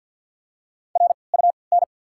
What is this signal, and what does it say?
Morse code, heard as a keyed tone, about 47 wpm, 690 Hz. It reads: R U N
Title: 3x Character Spacing - Top 300 Words - 50wpm